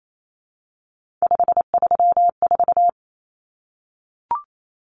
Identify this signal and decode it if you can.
Morse code, heard as a keyed tone, about 28 wpm, 700 Hz. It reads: 534
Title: Common Cut Numbers - 3-Digit 28wpm